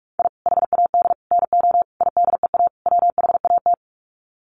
Morse code, heard as a keyed tone, about 44 wpm, 700 Hz. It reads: I HAD NO IDEA WHAT
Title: Sets of 5 Words Encore 45wpm